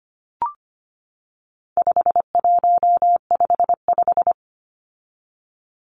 Morse code, heard as a keyed tone, about 25 wpm, 700 Hz. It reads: 5155
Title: Numbers - 4-Digit 25wpm